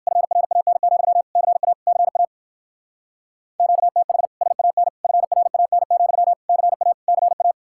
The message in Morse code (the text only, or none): FRAN, CA CA QTH SAN FRAN, CA CA